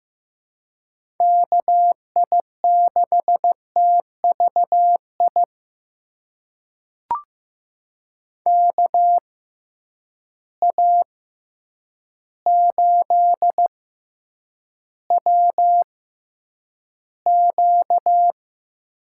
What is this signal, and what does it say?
Morse code, heard as a keyed tone, about 15 wpm, spaced as for 8 wpm, 700 Hz. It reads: KI6TVI KA8WQ
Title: USA Call Signs - 6 Characters - 6x Character Spacing - 15wpm